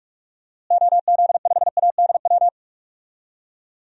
Morse code, heard as a keyed tone, about 41 wpm, 700 Hz. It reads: OZ5ADW
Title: DXCC-100 Call Signs - 6 Characters - Repeated 3x - 45wpm